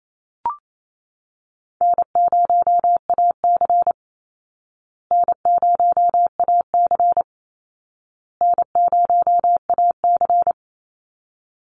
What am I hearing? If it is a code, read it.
Morse code, heard as a keyed tone, about 28 wpm, 700 Hz. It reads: N0AC N0AC N0AC
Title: Easy Call Signs Repeated 3x - 28wpm